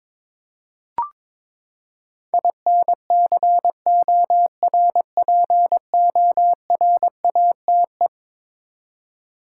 Morse code, heard as a keyed tone, about 22 wpm, 700 Hz. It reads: INCORPORATE
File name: practice / 22wpm / Eleven Letter Words - 22wpm